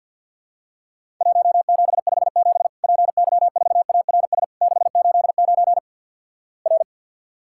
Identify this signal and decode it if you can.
Morse code, heard as a keyed tone, about 50 wpm, 700 Hz. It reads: 1757 PY4ARS 689 R
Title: Worked All Europe DX Contest - QTC Exchange - 50wpm